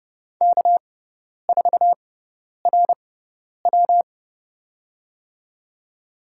Morse code, heard as a keyed tone, about 30 wpm, spaced as for 10 wpm, 700 Hz. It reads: K4RW
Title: USA Call Signs - 4 Characters - 6x Character Spacing - 30wpm